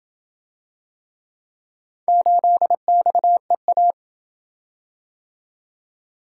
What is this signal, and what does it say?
Morse code, heard as a keyed tone, about 27 wpm, 700 Hz. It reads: 8XEA